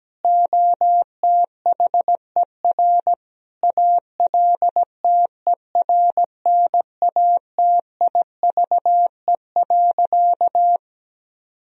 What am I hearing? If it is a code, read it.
Morse code, heard as a keyed tone, about 17 wpm, 700 Hz. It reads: OTHER ALTERNATIVE.
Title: Sentences from Top 1000 Words - Review Part 4 - 17wpm